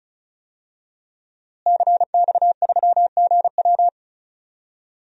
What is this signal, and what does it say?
Morse code, heard as a keyed tone, about 34 wpm, 700 Hz. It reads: CX3GW
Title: DXCC-100 Call Signs - 5 Characters - Repeated 2x - 35wpm